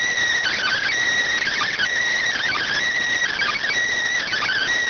Emergency Alert System (SAME) tones (header bursts), heard as harsh buzzing bursts. It is heard from 0.4 to 4.7 s.